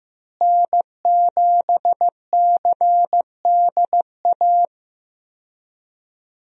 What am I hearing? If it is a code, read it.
Morse code, heard as a keyed tone, about 15 wpm, 700 Hz. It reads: N7CDA
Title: USA Call Signs - 5 Characters - Repeated 2x - 15wpm